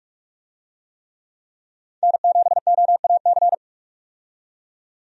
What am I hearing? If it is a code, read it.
Morse code, heard as a keyed tone, about 44 wpm, 700 Hz. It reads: N7OAC
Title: USA Call Signs - 5 Characters - 45wpm